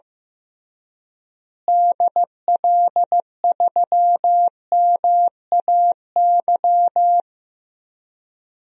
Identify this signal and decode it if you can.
Morse code, heard as a keyed tone, about 16 wpm, 700 Hz. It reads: DL3MAY
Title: DXCC-100 Call Signs - 6 Characters - Repeated 3x - 15wpm